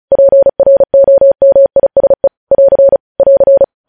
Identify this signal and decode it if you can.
Morse code, heard as a keyed tone, about 36 wpm, 550 Hz. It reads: PROMISE <AR> <AR>
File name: podcast / News headlines-1x-35x35